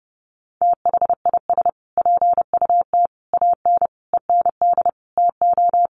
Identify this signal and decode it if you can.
Morse code, heard as a keyed tone, about 30 wpm, 700 Hz. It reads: THIS PUT AN END TO
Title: Sentences from Top 1100 Words 30wpm